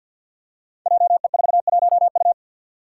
Morse code, heard as a keyed tone, about 51 wpm, 700 Hz. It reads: JE41U
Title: Five Challenging Letter-Number Combinations 50wpm